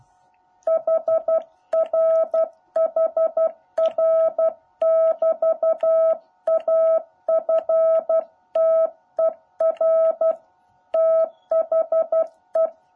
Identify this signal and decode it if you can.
Morse code, heard as a keyed tone, about 12 wpm, 650 Hz. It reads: HR HR<BT>AFTER THE
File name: jh8cbh / 2021cw